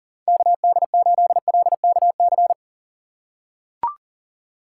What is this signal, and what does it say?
Morse code, heard as a keyed tone, about 40 wpm, 700 Hz. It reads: KD8LKC
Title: USA Call Signs - 4x Character Spacing - 40wpm